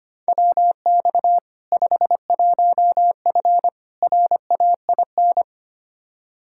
Morse code, heard as a keyed tone, about 25 wpm, 700 Hz. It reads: WX 51F RAIN
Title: K1USN SST QSO Practice 25wpm